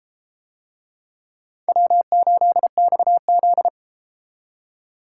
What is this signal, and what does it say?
Morse code, heard as a keyed tone, about 33 wpm, 700 Hz. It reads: W8XZ